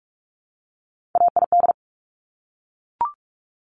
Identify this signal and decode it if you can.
Morse code, heard as a keyed tone, about 46 wpm, 700 Hz. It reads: AID
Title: Top Three Letter Combinations in Words 45wpm